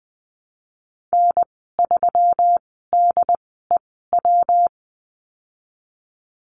Morse code, heard as a keyed tone, about 20 wpm, spaced as for 14 wpm, 700 Hz. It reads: N3DEW
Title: 122 Lesson 20g - Number 3 - Call Signs 20wpm